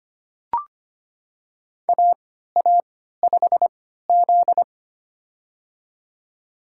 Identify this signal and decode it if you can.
Morse code, heard as a keyed tone, about 25 wpm, spaced as for 14 wpm, 700 Hz. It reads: AA5Z
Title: USA Call Signs - 4 Characters - 3x Character Spacing - 25wpm